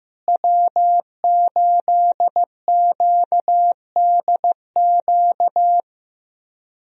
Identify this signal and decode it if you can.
Morse code, heard as a keyed tone, about 15 wpm, 700 Hz. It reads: W8QDQ
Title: USA Call Signs - 5 Characters - Repeated 2x - 15wpm